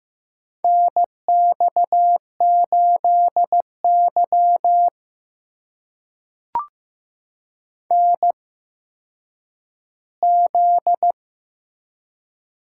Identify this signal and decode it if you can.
Morse code, heard as a keyed tone, about 15 wpm, spaced as for 9 wpm, 700 Hz. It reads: NX8Y NZ
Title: USA Call Signs - 4 Characters - 8x Character Spacing - 15wpm